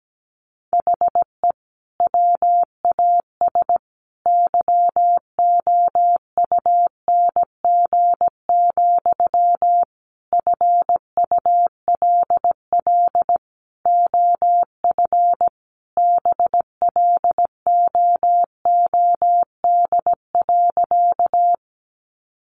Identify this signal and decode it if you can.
Morse code, heard as a keyed tone, about 17 wpm, 700 Hz. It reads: HE WAS YOUNG, FULL OF BLOOD.
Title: Sentences from Top 1100 Words 17wpm